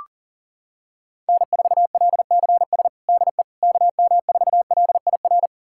Morse code, heard as a keyed tone, about 40 wpm, 700 Hz. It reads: N4LCS DE KM4LIR
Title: USA Callsign DE Callsign 40wpm